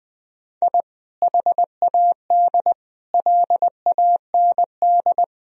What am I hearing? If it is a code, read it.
Morse code, heard as a keyed tone, about 20 wpm, 700 Hz. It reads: I HAD LAND
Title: Sentences from Top 1000 Words - Review Part 1 - 20wpm